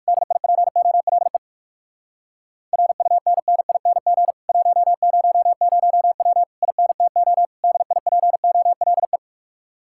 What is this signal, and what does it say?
Morse code, heard as a keyed tone, about 59 wpm, 700 Hz. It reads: DIPOLE RUNNING 100W INTO DIPOLE